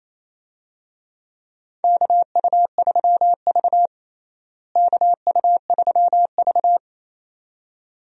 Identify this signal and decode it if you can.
Morse code, heard as a keyed tone, about 28 wpm, 700 Hz. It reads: KU3V KU3V
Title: USA Call Signs - 4 Characters - Repeated 2x - 28wpm